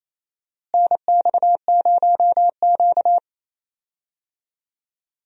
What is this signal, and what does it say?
Morse code, heard as a keyed tone, about 27 wpm, 700 Hz. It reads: NX0Q
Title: USA Call Signs - 4 Characters - Repeated 3x - 28wpm